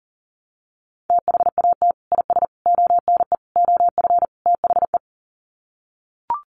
Morse code, heard as a keyed tone, about 40 wpm, 700 Hz. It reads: THAT IS ONE OF THE
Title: Sets of 5 Words Encore 40wpm